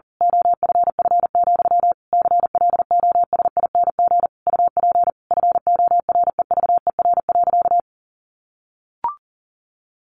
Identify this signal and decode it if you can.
Morse code, heard as a keyed tone, about 40 wpm, 700 Hz. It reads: OFF, CLOSING UP FOREVER.
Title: A Sound Of Thunder 40wpm